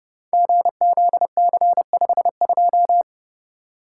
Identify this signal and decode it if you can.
Morse code, heard as a keyed tone, about 30 wpm, 700 Hz. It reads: GZC52
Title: Code Groups - 5 Letter:Numbers - 30wpm